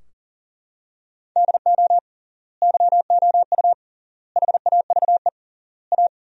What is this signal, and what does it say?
Morse code, heard as a keyed tone, about 40 wpm, 700 Hz. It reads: DO YOU HAVE A
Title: 3x Word-Spacing - Sentences from Top 300 Words - 40wpm